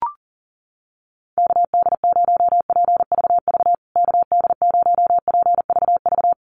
Morse code, heard as a keyed tone, about 40 wpm, 700 Hz. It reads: KD0PVV KD0PVV
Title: USA Call Signs Repeated 3x 40wpm